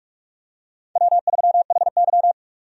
Morse code, heard as a keyed tone, about 46 wpm, 700 Hz. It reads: W3HY
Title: USA Call Signs - 4 Characters - 45wpm